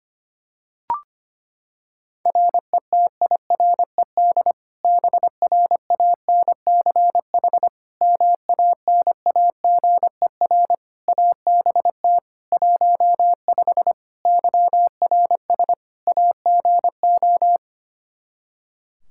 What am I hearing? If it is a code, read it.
Morse code, heard as a keyed tone, about 25 wpm, 700 Hz. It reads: RETIRED BRANCH MANAGER ABT 15 YRS AGO
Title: Practice QSO - Exchange 3 - More Info - 25wpm